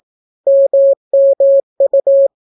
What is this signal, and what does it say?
Morse code, heard as a keyed tone, about 18 wpm, 550 Hz. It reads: MMU